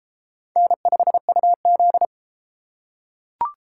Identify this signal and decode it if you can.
Morse code, heard as a keyed tone, about 32 wpm, 700 Hz. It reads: N5UZ